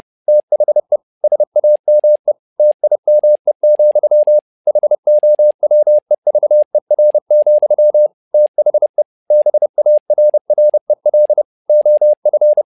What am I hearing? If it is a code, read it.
Morse code, heard as a keyed tone, about 30 wpm, 600 Hz. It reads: THE SAME TIME, HOWEVER, THE BARREL OF
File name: learning center / tower trouble / tower trouble 30wpm